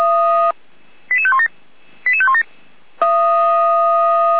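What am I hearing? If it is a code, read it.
Five-tone selective call (ZVEI), heard as quick runs of five tones, tones of about 70 ms. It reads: 88517 88517
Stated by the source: ZVEI five-tone selective call received by SDR on channel 455, recorded for feuerwehr-orlamuende.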